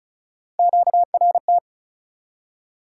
Morse code, heard as a keyed tone, about 49 wpm, spaced as for 42 wpm, 700 Hz. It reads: QRT
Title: Speed-Racing Top 180 QSO Elements 35wpm